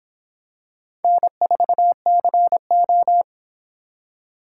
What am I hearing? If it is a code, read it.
Morse code, heard as a keyed tone, about 26 wpm, 700 Hz. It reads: N4CO